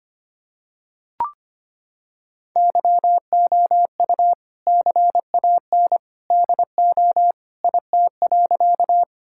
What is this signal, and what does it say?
Morse code, heard as a keyed tone, about 25 wpm, 700 Hz. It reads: YOU CAN DO IT.
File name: practice / 25wpm / Sentences with 2-Word Phrases - 25wpm